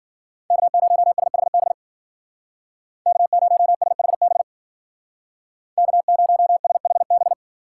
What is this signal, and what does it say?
Morse code, heard as a keyed tone, about 53 wpm, 700 Hz. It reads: K0SHB K0SHB K0SHB